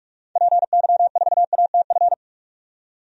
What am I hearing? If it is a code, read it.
Morse code, heard as a keyed tone, about 42 wpm, 700 Hz. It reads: PY4ATF
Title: DXCC-100 Call Signs - 6 Characters - 45wpm